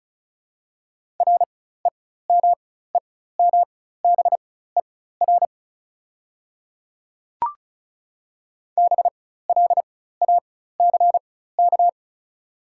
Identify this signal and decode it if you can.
Morse code, heard as a keyed tone, about 36 wpm, spaced as for 16 wpm, 700 Hz. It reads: REMEMBER BLACK
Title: 4x Character Spacing - Top 500 Words - 35wpm